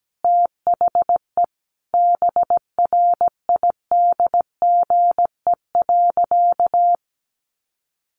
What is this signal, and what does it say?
Morse code, heard as a keyed tone, about 17 wpm, 700 Hz. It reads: THE BRIDGE.
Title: Sentences from Top 1500 Words 17wpm